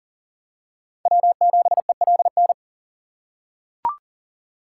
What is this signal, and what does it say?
Morse code, heard as a keyed tone, about 40 wpm, 700 Hz. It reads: W7ELN
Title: USA Call Signs - 5 Characters - 6x Character Spacing - 40wpm